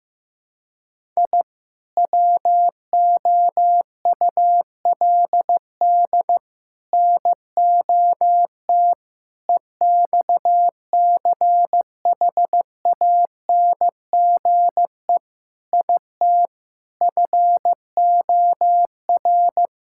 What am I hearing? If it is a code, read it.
Morse code, heard as a keyed tone, about 15 wpm, 700 Hz. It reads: I WOULD NOT EXCHANGE IT FOR
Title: Sentences from Top 1000 Words - Review Part 6 - 15wpm